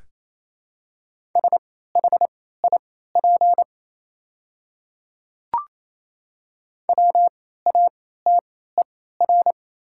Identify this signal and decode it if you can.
Morse code, heard as a keyed tone, about 28 wpm, spaced as for 16 wpm, 700 Hz. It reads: SHIP WATER
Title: 3x Character Spacing - Top 500 Words - 28wpm